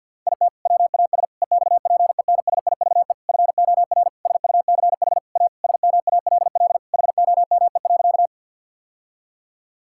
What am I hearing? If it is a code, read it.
Morse code, heard as a keyed tone, about 52 wpm, 700 Hz. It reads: IT WAS EXPENSIVE FOR SUCH A SMALL HOME.
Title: Sentences from Top 1000 Words - Review Part 6 - 50wpm